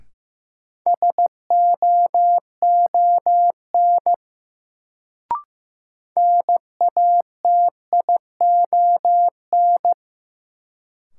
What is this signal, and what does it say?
Morse code, heard as a keyed tone, about 15 wpm, 700 Hz. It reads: SOON NATION
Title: Sets of 3 Words - New Word Drill - 15wpm